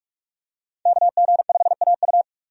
Morse code, heard as a keyed tone, about 46 wpm, 700 Hz. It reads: KG5AU
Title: USA Call Signs - 5 Characters - 45wpm